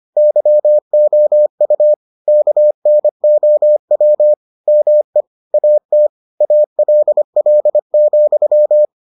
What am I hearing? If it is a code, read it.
Morse code, heard as a keyed tone, about 25 wpm, 600 Hz. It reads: YOU KNOW ME AT ALL,